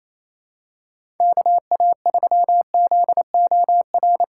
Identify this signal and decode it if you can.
Morse code, heard as a keyed tone, about 28 wpm, 700 Hz. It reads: KA3ZOR
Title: USA Callsign DE Callsign - 6 Characters - 28wpm